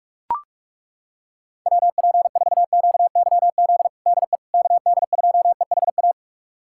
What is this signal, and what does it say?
Morse code, heard as a keyed tone, about 41 wpm, 700 Hz. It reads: WP4QYZ DE KD2EHA
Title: USA Callsign DE Callsign - 6 Characters - 45wpm